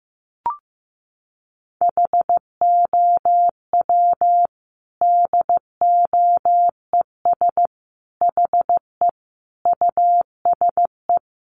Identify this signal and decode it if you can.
Morse code, heard as a keyed tone, about 15 wpm, 700 Hz. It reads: HOW DOES HE USE
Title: Sentences from Top 200 Words 15wpm